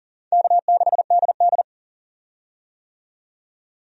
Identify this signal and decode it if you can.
Morse code, heard as a keyed tone, about 40 wpm, 700 Hz. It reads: K6DD
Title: USA Call Signs - 4 Characters - 40wpm